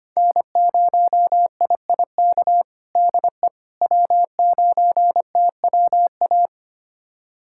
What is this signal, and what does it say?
Morse code, heard as a keyed tone, about 25 wpm, 700 Hz. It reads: N0IIK DE W9TWA
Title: USA Callsign DE Callsign - 5 Characters - 25wpm